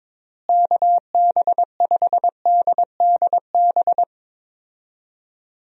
Morse code, heard as a keyed tone, about 22 wpm, 700 Hz. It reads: KB5DDB
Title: USA Call Signs - 6 Characters - Repeated 2x - 22wpm